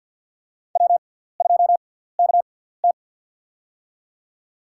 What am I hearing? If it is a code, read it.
Morse code, heard as a keyed tone, about 53 wpm, spaced as for 17 wpm, 700 Hz. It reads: W2KT
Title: USA Call Signs - 6x Character Spacing - 50wpm